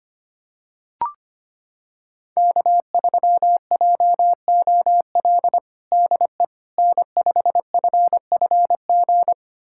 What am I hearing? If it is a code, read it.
Morse code, heard as a keyed tone, about 25 wpm, 700 Hz. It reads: K3JOL DE N5FFG
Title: USA Callsign DE Callsign - 5 Characters - 25wpm